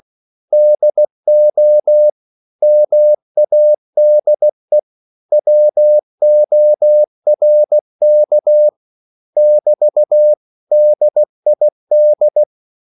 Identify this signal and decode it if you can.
Morse code, heard as a keyed tone, about 16 wpm, 600 Hz. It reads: DO MADE WORK <BT> DID
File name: CW / Practice / 500MostCommonEnglishWords16